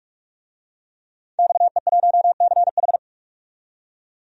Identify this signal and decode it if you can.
Morse code, heard as a keyed tone, about 43 wpm, 700 Hz. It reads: XE1CH
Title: DXCC-100 Call Signs - 5 Characters - 45wpm